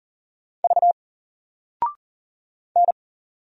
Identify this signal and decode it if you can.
Morse code, heard as a keyed tone, about 40 wpm, 700 Hz. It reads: V N
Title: Single Letter-Number 40wpm